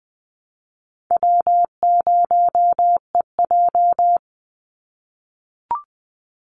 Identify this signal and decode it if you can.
Morse code, heard as a keyed tone, about 20 wpm, 700 Hz. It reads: W0EJ
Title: Easy Call Signs - 20wpm